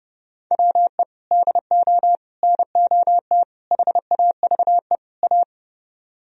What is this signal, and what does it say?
Morse code, heard as a keyed tone, about 28 wpm, 700 Hz. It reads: WE DO NOT HAVE A
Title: Speed-Racing - Sets of 5 Words 30wpm